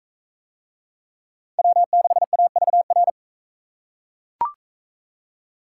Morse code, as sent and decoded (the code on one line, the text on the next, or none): .-- -.... .- ...- .-.
W6AVR